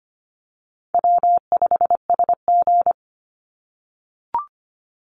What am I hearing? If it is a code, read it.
Morse code, heard as a keyed tone, about 25 wpm, 700 Hz. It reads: W5SG
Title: Easy Call Signs - 25wpm